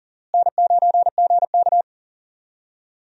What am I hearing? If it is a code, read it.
Morse code, heard as a keyed tone, about 40 wpm, 700 Hz. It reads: N9GK
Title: USA Call Signs - 4 Characters - Repeated 3x - 40wpm